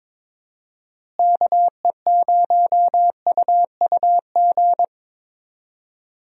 Morse code, heard as a keyed tone, about 21 wpm, 700 Hz. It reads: KE0UUG
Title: USA Call Signs - 6 Characters - Repeated 2x - 22wpm